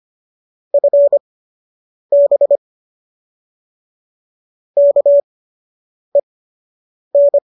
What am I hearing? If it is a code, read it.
Morse code, heard as a keyed tone, about 25 wpm, spaced as for 8 wpm, 570 Hz. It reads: FB KEN
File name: qso003 08